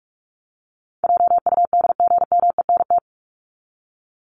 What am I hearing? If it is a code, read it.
Morse code, heard as a keyed tone, about 44 wpm, 700 Hz. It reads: JUDGMENT
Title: Top 1751-2000 Words 45wpm